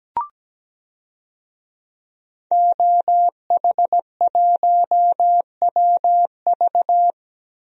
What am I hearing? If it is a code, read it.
Morse code, heard as a keyed tone, about 17 wpm, 700 Hz. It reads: OH1WV
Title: DXCC-100 Call Signs - 5 Characters - 17wpm